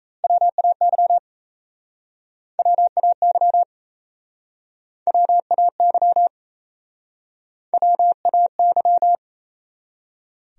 Morse code, heard as a keyed tone, about 37 wpm, 700 Hz. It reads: WAY WAY WAY WAY